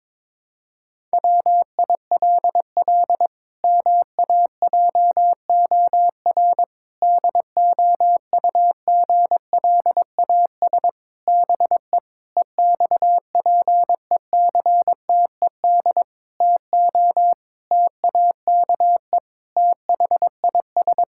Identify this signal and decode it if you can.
Morse code, heard as a keyed tone, about 22 wpm, 700 Hz. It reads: WILL MAJOR DOUGLAS BE EXPECTED TO TAKE THIS
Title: Pangram Challenge v2 - 22wpm